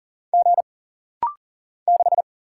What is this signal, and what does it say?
Morse code, heard as a keyed tone, about 40 wpm, 700 Hz. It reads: G 6
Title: Single Letter-Number Alt - Rapid-Fire 40wpm